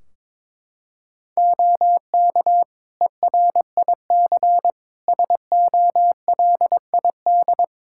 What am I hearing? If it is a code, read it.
Morse code, heard as a keyed tone, about 22 wpm, 700 Hz. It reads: OK ERIC SOLID
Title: Practice QSO - Exchange 3 - More Info - 22wpm